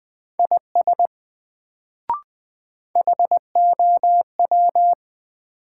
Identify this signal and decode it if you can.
Morse code, heard as a keyed tone, about 20 wpm, 700 Hz. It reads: IS HOW
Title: Sets of 2 Words - New Word Drill - 20wpm